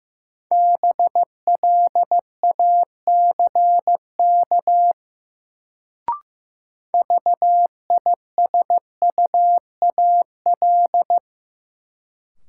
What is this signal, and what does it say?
Morse code, heard as a keyed tone, about 15 wpm, 700 Hz. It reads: BLACK VISUAL
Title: Sets of 5 Words Encore - New Word Drill - 15wpm